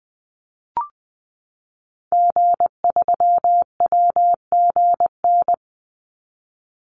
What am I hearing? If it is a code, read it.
Morse code, heard as a keyed tone, about 20 wpm, 700 Hz. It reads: G3WGN
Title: Call Signs - 20wpm